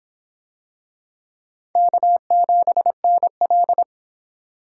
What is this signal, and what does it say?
Morse code, heard as a keyed tone, about 25 wpm, 700 Hz. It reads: K7NL